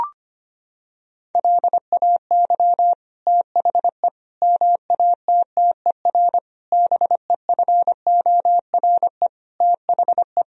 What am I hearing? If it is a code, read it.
Morse code, heard as a keyed tone, about 25 wpm, 700 Hz. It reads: LAY THE MATTER BEFORE THE
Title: Sentences from Top 1000 Words - Review Part 4 - 25wpm